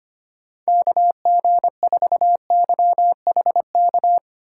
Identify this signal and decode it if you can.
Morse code, heard as a keyed tone, about 25 wpm, 700 Hz. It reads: KG4YHK
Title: USA Call Signs - 6 Characters - 8x Character Spacing - 25wpm